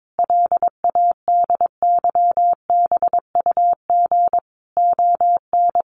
Morse code, heard as a keyed tone, about 22 wpm, 700 Hz. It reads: LADYBUG ON
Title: Additional Easy Phrases 22wpm